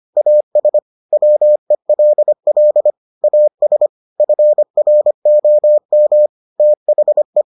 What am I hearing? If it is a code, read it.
Morse code, heard as a keyed tone, about 25 wpm, 600 Hz. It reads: AS WELL AS FROM THE